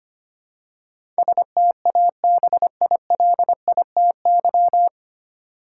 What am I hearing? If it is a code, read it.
Morse code, heard as a keyed tone, about 25 wpm, 700 Hz. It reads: STABILITY
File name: practice / 25wpm / 9 Letter Words - First 3 Letters Given - 25wpm